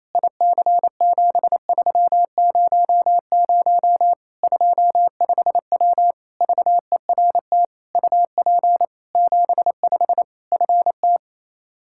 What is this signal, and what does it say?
Morse code, heard as a keyed tone, about 27 wpm, 700 Hz. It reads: IC7300 25W VERT UP 75 FT
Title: Practice QSO - Exchange 2 - Info - 28wpm